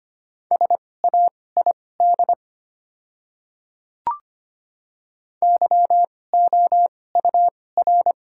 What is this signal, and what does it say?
Morse code, heard as a keyed tone, about 25 wpm, spaced as for 18 wpm, 700 Hz. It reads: SAID YOUR
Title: 2x Character Spacing - Top 100 Words - 25wpm